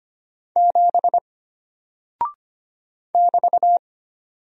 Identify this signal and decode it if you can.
Morse code, heard as a keyed tone, about 25 wpm, 700 Hz. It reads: 7 <BT>
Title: Single Character (Letter:Number:Punctuation:Prosign) Alt 25wpm